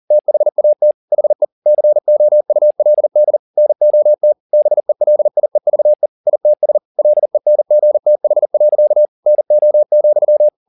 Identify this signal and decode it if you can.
Morse code, heard as a keyed tone, about 40 wpm, 600 Hz. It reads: THAT HE COULD NOT BELIEVE ITS LENGTH. NO,
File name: the old man and the sea 2 40wpm